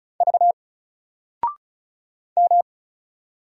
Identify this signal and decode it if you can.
Morse code, heard as a keyed tone, about 36 wpm, 700 Hz. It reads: V M
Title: Single Character (Letter:Number:Punctuation:Prosign) 35wpm